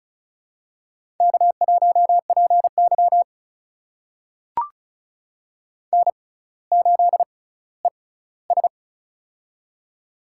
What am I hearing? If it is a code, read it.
Morse code, heard as a keyed tone, about 36 wpm, spaced as for 18 wpm, 700 Hz. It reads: K1PY N8ES
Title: USA Call Signs - 4 Characters - 6x Character Spacing - 35wpm